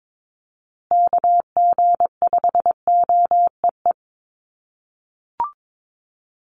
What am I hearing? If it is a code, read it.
Morse code, heard as a keyed tone, about 22 wpm, 700 Hz. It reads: KG5OEE
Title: USA Call Signs Repeated 2x 22wpm